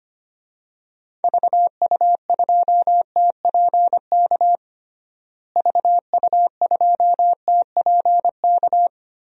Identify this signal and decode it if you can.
Morse code, heard as a keyed tone, about 25 wpm, 700 Hz. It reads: VU2TPK VU2TPK
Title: DXCC-100 Call Signs - 6 Characters - Repeated 3x - 25wpm